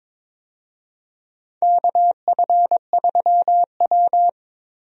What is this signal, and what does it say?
Morse code, heard as a keyed tone, about 22 wpm, 700 Hz. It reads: KF3W
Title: USA Call Signs - 4 Characters - 22wpm